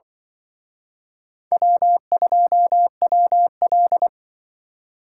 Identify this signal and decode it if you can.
Morse code, heard as a keyed tone, about 24 wpm, 700 Hz. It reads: W2WL
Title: Speed Racing - USA 1x2 Call Signs 20wpm